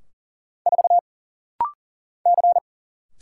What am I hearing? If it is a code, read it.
Morse code, heard as a keyed tone, about 40 wpm, 700 Hz. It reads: 4 C
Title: Single Letter-Number Alt - Rapid-Fire 40wpm